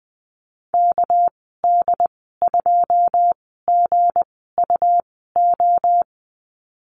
Morse code, heard as a keyed tone, about 20 wpm, 700 Hz. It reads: K D 2 G U O
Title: 200 Lesson 32g - Letter k - Call Signs 20wpm